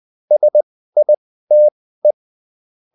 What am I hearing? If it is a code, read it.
Morse code, heard as a keyed tone, about 20 wpm, spaced as for 14 wpm, 600 Hz. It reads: SITE